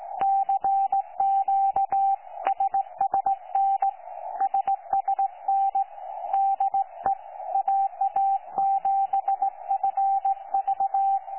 Morse code, heard as a keyed tone, about 18 wpm, 780 Hz. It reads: CQ SSN SSN DE AA7FV
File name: 12 45Z 18088.1kHz